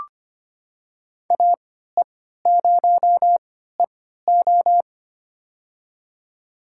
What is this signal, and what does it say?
Morse code, heard as a keyed tone, about 25 wpm, spaced as for 14 wpm, 700 Hz. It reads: AE0EO
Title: USA Call Signs - 5 Characters - 3x Character Spacing - 25wpm